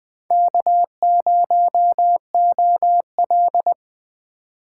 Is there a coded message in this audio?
Yes: Morse code from 0.3 to 3.7 s, heard as a keyed tone, about 21 wpm, 700 Hz. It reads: K0OL